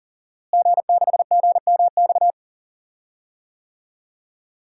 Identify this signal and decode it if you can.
Morse code, heard as a keyed tone, about 39 wpm, 700 Hz. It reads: G6GMX
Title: DXCC-100 Call Signs - 5 Characters - Repeated 3x - 40wpm